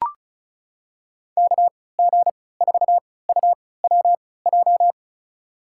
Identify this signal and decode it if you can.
Morse code, heard as a keyed tone, about 34 wpm, spaced as for 20 wpm, 700 Hz. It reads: KG4UWJ
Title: USA Call Signs - 6 Characters - 3x Character Spacing - 35wpm